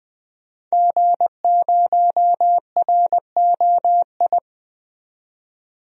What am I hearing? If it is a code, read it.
Morse code, heard as a keyed tone, about 20 wpm, 700 Hz. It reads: G0ROI
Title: DXCC-100 Call Signs - 5 Characters - 20wpm